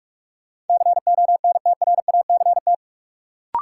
Morse code, heard as a keyed tone, about 46 wpm, 700 Hz. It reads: CONTRACT
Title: Sentences from Top 1400 Words - New Word Drill Alt - 45wpm